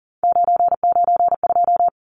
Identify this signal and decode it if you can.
Morse code, heard as a keyed tone, about 40 wpm, 700 Hz. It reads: 992
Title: Numbers - 3-Digit 40wpm